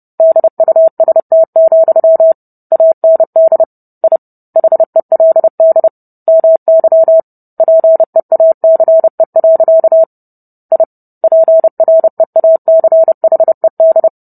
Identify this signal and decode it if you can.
Morse code, heard as a keyed tone, about 30 wpm, 650 Hz. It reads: DUST, AND I HELD MY PEACE. I PREACHED